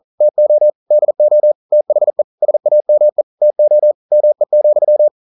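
Morse code, heard as a keyed tone, about 42 wpm, 600 Hz. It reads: TO DO THE SAME TO ME,
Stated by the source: Morse code practice at 41 wpm, from CW med Gnister.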